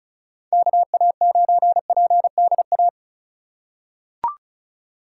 Morse code, heard as a keyed tone, about 36 wpm, 700 Hz. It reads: KA9PDA
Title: USA Call Signs - 6 Characters - 2x Character Spacing - 35wpm